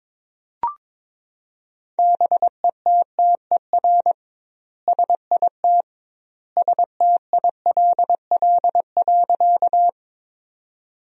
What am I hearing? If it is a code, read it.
Morse code, heard as a keyed tone, about 22 wpm, 700 Hz. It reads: BETTER SIT STILL.
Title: 2x Word-Spacing - Sentences from Top 500 Words - 22wpm